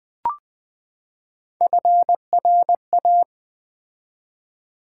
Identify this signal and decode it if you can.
Morse code, heard as a keyed tone, about 20 wpm, 700 Hz. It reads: FRA
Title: Build A Word - 3 Letters At A Time - 9-Letter Words - 20wpm